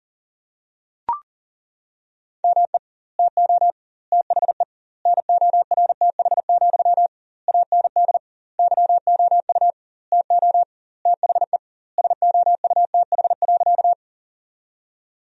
Morse code, heard as a keyed tone, about 40 wpm, 700 Hz. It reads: ME TO THE NORTH, AND YOU TO THE SOUTH.
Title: 2x Word-Spacing - Sentences from Top 400 Words - 40wpm